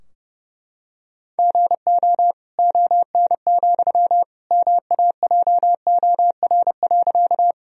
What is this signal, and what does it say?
Morse code, heard as a keyed tone, about 32 wpm, 700 Hz. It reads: GO ON, MAJOR.
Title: Sentences from Top 1000 Words - Review Part 4 - 30wpm